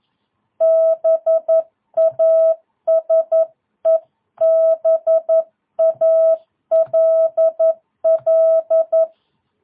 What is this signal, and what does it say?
Morse code, heard as a keyed tone, about 11 wpm, 650 Hz. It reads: BASEBALL
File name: jh8cbh / 2022cw